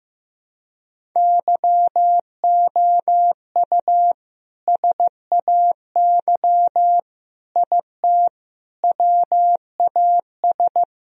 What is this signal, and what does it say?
Morse code, heard as a keyed tone, about 15 wpm, 700 Hz. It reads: YOU SAY IT WAS
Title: Sentences from Top 1000 Words - Review Part 1 - 15wpm